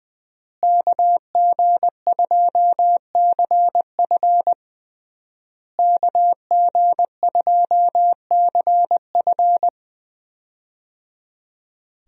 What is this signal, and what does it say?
Morse code, heard as a keyed tone, about 21 wpm, 700 Hz. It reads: KG2CF KG2CF
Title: USA Call Signs - 5 Characters - Repeated 3x - 20wpm